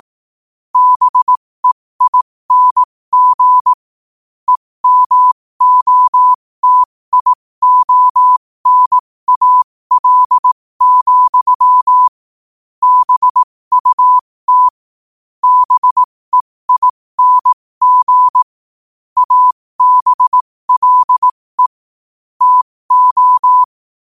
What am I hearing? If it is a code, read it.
Morse code, heard as a keyed tone, about 18 wpm, spaced as for 15 wpm, 1000 Hz. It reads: BEING EMOTIONAL, BUT BEING ABLE TO
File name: qotdfiles / 15wpm